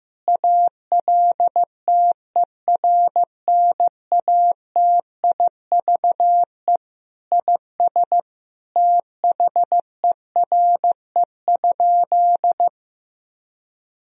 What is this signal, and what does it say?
Morse code, heard as a keyed tone, about 16 wpm, 700 Hz. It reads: ALTERNATIVE IS THERE?
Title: Sentences from Top 1000 Words - Review Part 4 - 15wpm